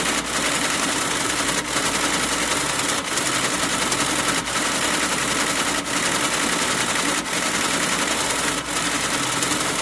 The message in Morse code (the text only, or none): none